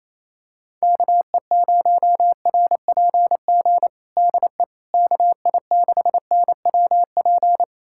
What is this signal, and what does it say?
Morse code, heard as a keyed tone, about 28 wpm, 700 Hz. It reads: KE0RPG DE KI6NWP
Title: USA Callsign DE Callsign - 6 Characters - 28wpm